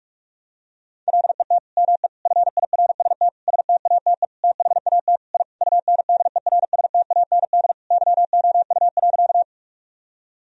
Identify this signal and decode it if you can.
Morse code, heard as a keyed tone, about 46 wpm, 700 Hz. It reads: LET ME FIRST STATE THAT I UNDERSTAND YOU.
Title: Sentences from Top 1000 Words - Review Part 2 - 45wpm